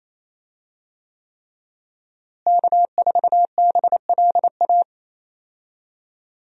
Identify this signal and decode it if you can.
Morse code, heard as a keyed tone, about 28 wpm, 700 Hz. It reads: K4BLA
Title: USA Call Signs - 5 Characters - Repeated 3x - 28wpm